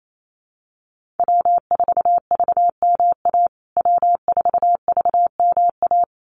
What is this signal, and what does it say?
Morse code, heard as a keyed tone, about 28 wpm, 700 Hz. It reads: W4VMA W4VMA
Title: USA Call Signs Repeated 2x 28wpm